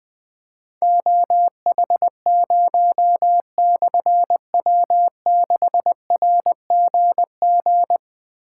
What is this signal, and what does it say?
Morse code, heard as a keyed tone, about 20 wpm, 700 Hz. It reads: OH0/W6RGG
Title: Difficult Call Signs 20wpm